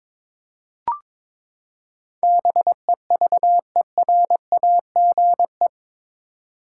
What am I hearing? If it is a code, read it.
Morse code, heard as a keyed tone, about 22 wpm, 700 Hz. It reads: BEVERAGE
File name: practice / 22wpm / Eight Letter Words - 22wpm